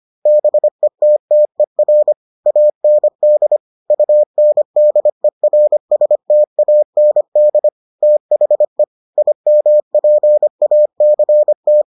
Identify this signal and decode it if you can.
Morse code, heard as a keyed tone, about 25 wpm, 600 Hz. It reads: BETTER AND UNDERSTAND THE IMPACT